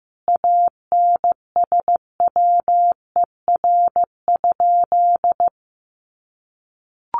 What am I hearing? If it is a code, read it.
Morse code, heard as a keyed tone, about 15 wpm, 700 Hz. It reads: ANSWER?
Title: Sentences from Top 300 Words 15wpm